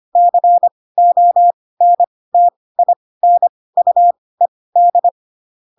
Morse code, heard as a keyed tone, about 25 wpm, spaced as for 18 wpm, 700 Hz. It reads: CONTINUED
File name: Pinocchio / Pinocchio ch4 12 25 ik1hge